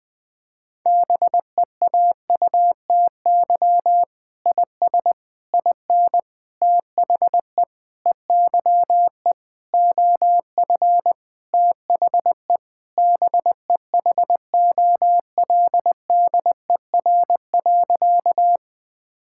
Morse code, heard as a keyed tone, about 21 wpm, 700 Hz. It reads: BEAUTY IS IN THE EYE OF THE BEHOLDER.
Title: Sentences with 7-Word Phrases - 20wpm